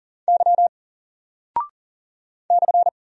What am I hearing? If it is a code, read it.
Morse code, heard as a keyed tone, about 40 wpm, 700 Hz. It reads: Y /
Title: Single Character (Letter:Number:Punctuation:Prosign) Alt 40wpm